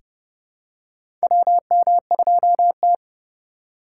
Morse code, heard as a keyed tone, about 30 wpm, 700 Hz. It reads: WM2T
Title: USA Call Signs - 4 Characters - 4x Character Spacing - 30wpm